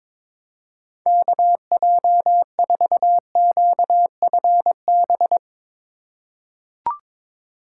Morse code, heard as a keyed tone, about 22 wpm, 700 Hz. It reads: KJ4QFB
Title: USA Call Signs - 6 Characters - 4x Character Spacing - 22wpm